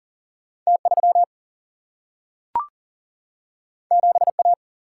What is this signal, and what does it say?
Morse code, heard as a keyed tone, about 40 wpm, 700 Hz. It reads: T3 7A
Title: Single-Letter Single-Number 40wpm